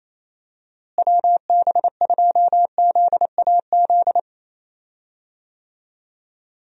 Morse code, heard as a keyed tone, about 28 wpm, 700 Hz. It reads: WB2ZAZ
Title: USA Call Signs - 6 Characters - Repeated 3x - 28wpm